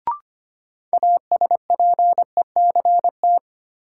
Morse code, heard as a keyed tone, about 25 wpm, 700 Hz. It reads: ASPECT
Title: Sentences from Top 1300 Words - New Word Drill Alt - 25wpm